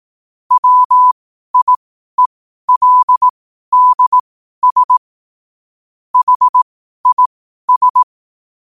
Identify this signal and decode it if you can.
Morse code, heard as a keyed tone, about 18 wpm, spaced as for 12 wpm, 1000 Hz. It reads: WIELDS HIS
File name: qotdfiles / 12wpm